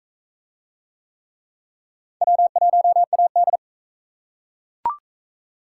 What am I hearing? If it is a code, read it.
Morse code, heard as a keyed tone, about 35 wpm, 700 Hz. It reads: W1AD
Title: Speed Racing - USA 1x2 Call Signs 28wpm